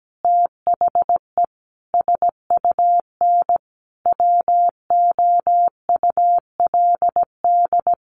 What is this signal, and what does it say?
Morse code, heard as a keyed tone, about 17 wpm, 700 Hz. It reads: THE SUN WOULD
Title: Sentences from Top 400 Words 17wpm